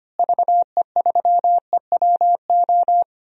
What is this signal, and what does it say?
Morse code, heard as a keyed tone, about 25 wpm, 700 Hz. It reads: VE3EWO
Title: DXCC-100 Call Signs - 6 Characters - Repeated 3x - 25wpm